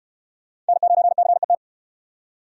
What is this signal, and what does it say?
Morse code, heard as a keyed tone, about 66 wpm, 700 Hz. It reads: N9CET